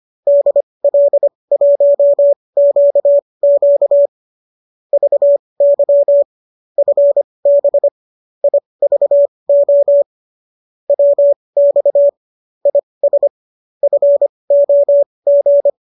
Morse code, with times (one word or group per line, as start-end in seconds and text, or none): 0.3-4.1 DL1QQ
4.9-6.2 VY
6.8-7.9 FB
8.4-10.0 IVO
10.9-12.1 WX
12.7-13.3 IS
13.8-15.7 FOG